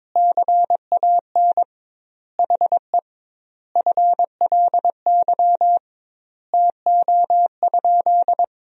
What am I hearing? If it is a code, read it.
Morse code, heard as a keyed tone, about 21 wpm, 700 Hz. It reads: CAN HE FLY TO?
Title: 2x Word-Spacing - Sentences from Top 500 Words - 22wpm